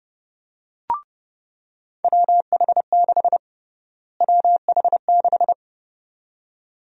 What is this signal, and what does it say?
Morse code, heard as a keyed tone, about 30 wpm, 700 Hz. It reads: WH6 WH6
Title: USA Call Sign Prefixes - Repeated 2x - 30wpm